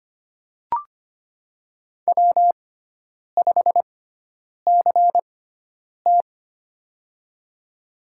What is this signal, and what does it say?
Morse code, heard as a keyed tone, about 25 wpm, spaced as for 9 wpm, 700 Hz. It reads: W5CT
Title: USA Call Signs - 4 Characters - 6x Character Spacing - 25wpm